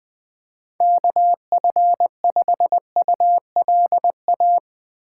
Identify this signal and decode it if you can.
Morse code, heard as a keyed tone, about 20 wpm, 700 Hz. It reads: KF5ULA
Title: USA Call Signs - 6 Characters - 3x Character Spacing - 20wpm